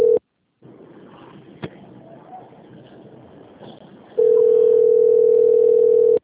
Ringback tone (North America), 440 + 480 Hz.
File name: out- - - - - .6608